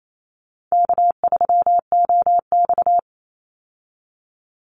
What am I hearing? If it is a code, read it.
Morse code, heard as a keyed tone, about 28 wpm, 700 Hz. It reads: K3OX
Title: Easy Call Signs Repeated 3x - 28wpm